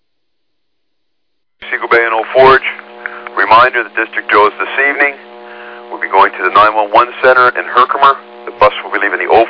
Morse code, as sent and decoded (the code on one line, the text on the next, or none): none